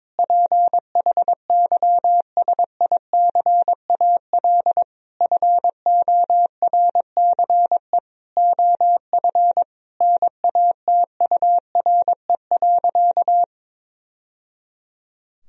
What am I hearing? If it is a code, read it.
Morse code, heard as a keyed tone, about 22 wpm, 700 Hz. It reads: PHYSICAL FORCE OF NATURE.
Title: Sentences from Top 1000 Words - Review Part 4 - 22wpm